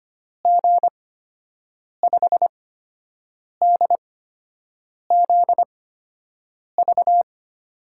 Morse code, heard as a keyed tone, about 25 wpm, spaced as for 7 wpm, 700 Hz. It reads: G5DZV
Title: USA Call Signs - 6 Characters - 8x Character Spacing - 25wpm